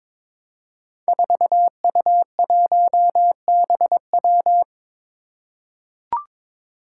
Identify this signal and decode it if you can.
Morse code, heard as a keyed tone, about 23 wpm, 700 Hz. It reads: 4U1BW
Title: Five Challenging Letter-Number Combinations 22wpm